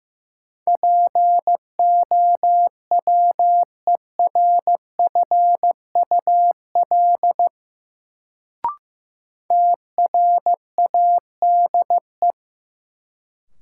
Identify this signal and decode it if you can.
Morse code, heard as a keyed tone, about 15 wpm, 700 Hz. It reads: POWERFUL TRADE
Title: Sets of 5 Words Encore - New Word Drill - 15wpm